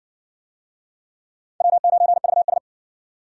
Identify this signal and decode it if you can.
Morse code, heard as a keyed tone, about 60 wpm, 700 Hz. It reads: W9VS